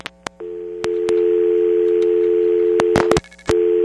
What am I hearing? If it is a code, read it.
Dial tone (North America), 350 + 440 Hz.
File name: TurtleCreekEWSD